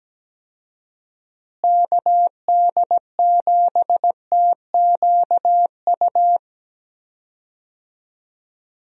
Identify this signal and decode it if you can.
Morse code, heard as a keyed tone, about 17 wpm, 700 Hz. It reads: KD7TQU
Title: USA Call Signs - 6 Characters - Repeated 3x - 17wpm